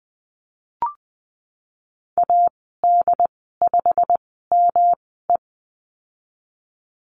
Morse code, heard as a keyed tone, about 20 wpm, spaced as for 15 wpm, 700 Hz. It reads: AD5ME
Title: 147 Lesson 24g - Letter y - Call Signs 20wpm